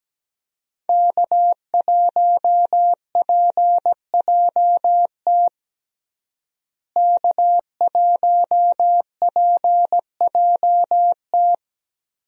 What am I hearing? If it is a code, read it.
Morse code, heard as a keyed tone, about 17 wpm, 700 Hz. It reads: K1PJT K1PJT
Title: USA Call Signs - 5 Characters - Repeated 3x - 17wpm